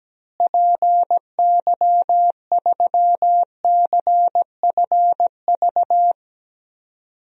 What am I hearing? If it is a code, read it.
Morse code, heard as a keyed tone, about 17 wpm, 700 Hz. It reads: PY3CFV
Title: DXCC-100 Call Signs - 6 Characters - Repeated 2x - 17wpm